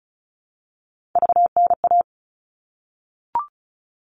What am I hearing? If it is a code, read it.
Morse code, heard as a keyed tone, about 34 wpm, 700 Hz. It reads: VNA
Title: All Cut Numbers - 3-Digit 35wpm